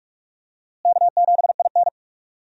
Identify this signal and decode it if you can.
Morse code, heard as a keyed tone, about 46 wpm, 700 Hz. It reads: K7IN
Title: USA Call Signs - 4 Characters - Repeated 2x - 45wpm